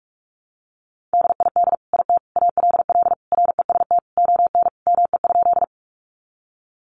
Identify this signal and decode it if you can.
Morse code, heard as a keyed tone, about 44 wpm, 700 Hz. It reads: DID IT ALL REST ON ME?
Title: Sentences from Top 500 Words 45wpm